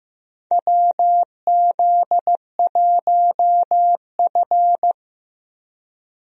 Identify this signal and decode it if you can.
Morse code, heard as a keyed tone, about 15 wpm, 700 Hz. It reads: WZ1F
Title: USA Call Signs - 4 Characters - Repeated 3x - 15wpm